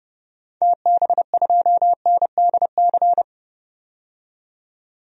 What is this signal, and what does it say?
Morse code, heard as a keyed tone, about 30 wpm, 700 Hz. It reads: TB2NDC
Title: DXCC-100 Call Signs - 6 Characters - 30wpm